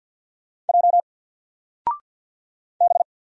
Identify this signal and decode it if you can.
Morse code, heard as a keyed tone, about 47 wpm, 700 Hz. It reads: J B
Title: Single Letter-Number 50wpm